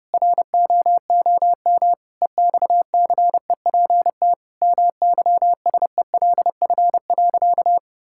Morse code, heard as a keyed tone, about 30 wpm, 700 Hz. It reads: ROOM EXCEPT MYSELF.
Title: Sentences from Top 1600 Words 30wpm